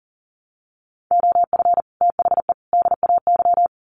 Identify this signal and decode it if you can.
Morse code, heard as a keyed tone, about 40 wpm, 700 Hz. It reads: OF THE DAY
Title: Sets of 3 Words 40wpm